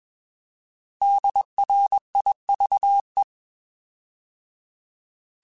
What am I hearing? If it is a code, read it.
Morse code, heard as a keyed tone, about 21 wpm, 800 Hz. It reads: DRIVE
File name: Alphabetical / Alphabetical mp3s / Ds @20